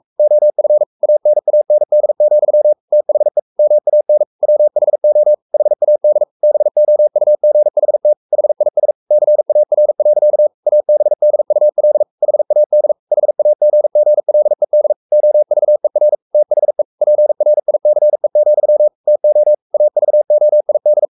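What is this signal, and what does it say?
Morse code, heard as a keyed tone, about 44 wpm, 600 Hz. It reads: OF ANAND, THE MAN WHO HAD BOUGHT HIS CAR. ABDUL HAD HAGGLED OVER THE PRICE, TO AVOID
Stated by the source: Morse code at 43 wpm sending Ken Follett's novel Never in English, from CW med Gnister.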